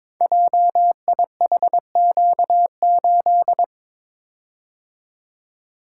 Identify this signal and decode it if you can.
Morse code, heard as a keyed tone, about 22 wpm, 700 Hz. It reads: JIHQ8
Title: Code Groups - 5 Letter:Numbers - 22wpm